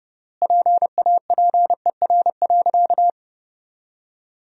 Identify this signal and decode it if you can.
Morse code, heard as a keyed tone, about 30 wpm, 700 Hz. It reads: PAPER.
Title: 3x Word-Spacing - Sentences from Top 500 Words - 30wpm